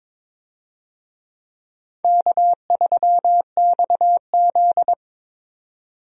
Morse code, heard as a keyed tone, about 21 wpm, 700 Hz. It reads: K3XZ
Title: USA Call Signs - 4 Characters - 22wpm